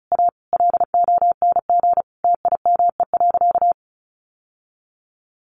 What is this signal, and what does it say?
Morse code, heard as a keyed tone, about 34 wpm, 700 Hz. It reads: A LONG TIME.
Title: Sentences from Top 1000 Words 35wpm